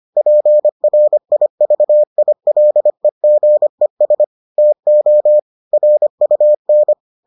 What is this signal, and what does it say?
Morse code, heard as a keyed tone, about 25 wpm, 600 Hz. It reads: PRIVILEGES TO RUN